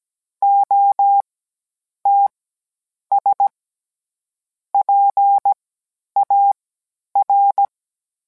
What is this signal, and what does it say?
Morse code, heard as a keyed tone, about 17 wpm, spaced as for 9 wpm, 800 Hz. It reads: OTS PAR